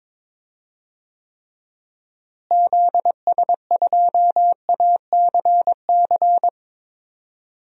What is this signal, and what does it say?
Morse code, heard as a keyed tone, about 22 wpm, 700 Hz. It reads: ZS2ACC
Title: DXCC-100 Call Signs - 6 Characters - Repeated 3x - 22wpm